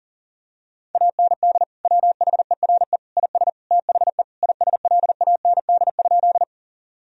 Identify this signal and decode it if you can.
Morse code, heard as a keyed tone, about 37 wpm, 700 Hz. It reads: AND WHERE IS THE ISLAND?
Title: Sentences from Top 1000 Words - Review Part 3 - 40wpm